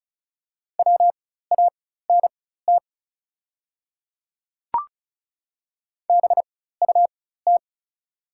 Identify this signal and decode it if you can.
Morse code, heard as a keyed tone, about 36 wpm, spaced as for 16 wpm, 690 Hz. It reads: WANT BUT
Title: 4x Character Spacing - Top 500 Words - 35wpm